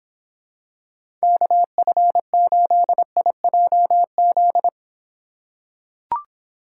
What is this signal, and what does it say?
Morse code, heard as a keyed tone, about 26 wpm, 700 Hz. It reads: KF8IJZ